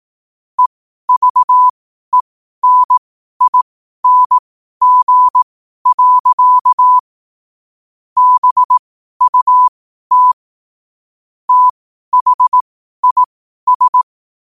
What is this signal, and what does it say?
Morse code, heard as a keyed tone, about 18 wpm, spaced as for 12 wpm, 1000 Hz. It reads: EVENING. BUT THIS
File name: 12wpm